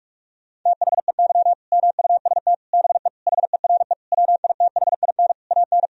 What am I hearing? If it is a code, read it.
Morse code, heard as a keyed tone, about 41 wpm, 700 Hz. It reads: THEY MUST BE HERE WITHIN AN
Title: Sentences from Top 1000 Words - Review Part 4 - 45wpm